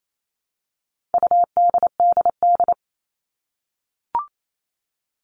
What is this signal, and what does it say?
Morse code, heard as a keyed tone, about 28 wpm, 700 Hz. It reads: UDDD
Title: All Cut Numbers - 4-Digit 28wpm